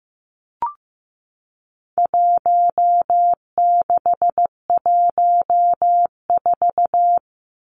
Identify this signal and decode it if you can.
Morse code, heard as a keyed tone, about 15 wpm, 700 Hz. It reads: 1614
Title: Numbers - Contest Style - 4-Digit 15wpm